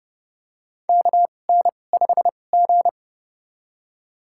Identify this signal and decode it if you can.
Morse code, heard as a keyed tone, about 32 wpm, spaced as for 22 wpm, 700 Hz. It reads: KN5G
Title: USA Call Signs - 4 Characters - 2x Character Spacing - 30wpm